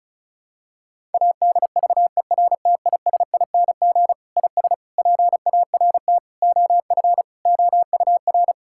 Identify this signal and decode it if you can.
Morse code, heard as a keyed tone, about 34 wpm, 700 Hz. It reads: ADVERTISING IS PART OF OUR
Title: Sentences from Top 1000 Words - Review Part 6 - 35wpm